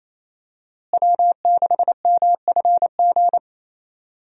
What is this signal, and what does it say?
Morse code, heard as a keyed tone, about 28 wpm, 700 Hz. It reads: W6MFG
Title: USA Call Signs - 5 Characters - Repeated 3x - 28wpm